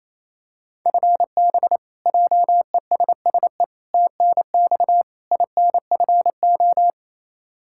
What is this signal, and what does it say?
Morse code, heard as a keyed tone, about 28 wpm, 700 Hz. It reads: FB JESSE TNX INFO
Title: K1USN SST QSO Practice 28wpm